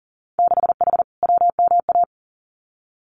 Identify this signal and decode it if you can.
Morse code, heard as a keyed tone, about 40 wpm, 700 Hz. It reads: 6H WMA
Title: Winter Field Day Exchanges Encore 40wpm